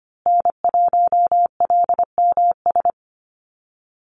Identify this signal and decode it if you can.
Morse code, heard as a keyed tone, about 25 wpm, 700 Hz. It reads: N1LMS
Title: USA Call Signs Repeated 2x 25wpm